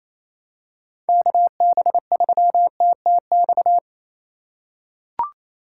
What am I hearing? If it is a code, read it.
Morse code, heard as a keyed tone, about 28 wpm, 700 Hz. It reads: KB3TTX
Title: USA Call Signs - 6 Characters - 2x Character Spacing - 28wpm